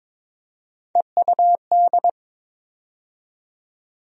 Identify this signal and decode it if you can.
Morse code, heard as a keyed tone, about 21 wpm, 700 Hz. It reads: EUD
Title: USA Call Sign Postfixes - 3 Letters 22wpm